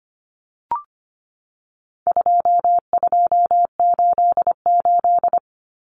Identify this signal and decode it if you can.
Morse code, heard as a keyed tone, about 25 wpm, 700 Hz. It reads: 2288
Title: Common Cut Numbers - 4-Digit 25wpm